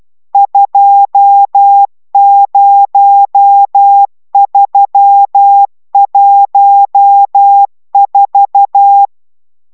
Morse code, heard as a keyed tone, about 12 wpm, 800 Hz. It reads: 20314